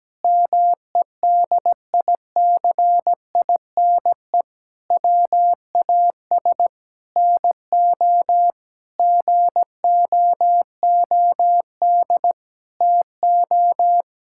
Morse code, heard as a keyed tone, about 17 wpm, 700 Hz. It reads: MEDICINE WAS NO GOOD TO
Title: Sentences from Top 1000 Words - Review Part 6 - 17wpm